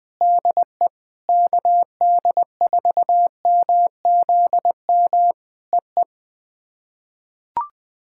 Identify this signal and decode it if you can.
Morse code, heard as a keyed tone, about 20 wpm, 700 Hz. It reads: DE KD4MZM EE
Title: POTA QSO Practice - 20wpm